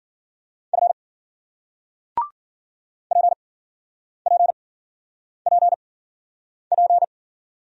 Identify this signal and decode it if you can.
Morse code, heard as a keyed tone, about 55 wpm, spaced as for 9 wpm, 700 Hz. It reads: F PPPP